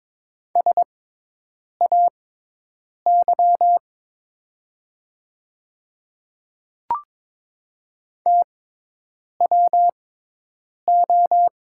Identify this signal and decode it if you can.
Morse code, heard as a keyed tone, about 22 wpm, spaced as for 8 wpm, 700 Hz. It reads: SAY TWO
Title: 6x Character Spacing - Top 200 Words - 22wpm